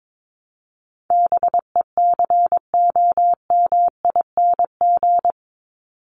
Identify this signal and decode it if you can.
Morse code, heard as a keyed tone, about 22 wpm, 700 Hz. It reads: BECOMING
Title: ING Suffix - Top 100 Words with Base Words 22wpm